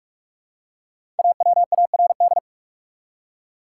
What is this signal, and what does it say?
Morse code, heard as a keyed tone, about 46 wpm, 700 Hz. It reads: AWARD
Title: Five Letter Words - 45wpm